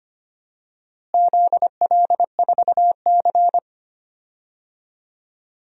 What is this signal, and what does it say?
Morse code, heard as a keyed tone, about 25 wpm, 700 Hz. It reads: ZL4C
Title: DXCC-100 Call Signs - 4 Characters - Repeated 2x - 25wpm